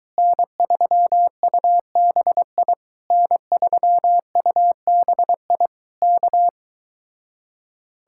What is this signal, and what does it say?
Morse code, heard as a keyed tone, about 23 wpm, 700 Hz. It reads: N3UBI N3UBI K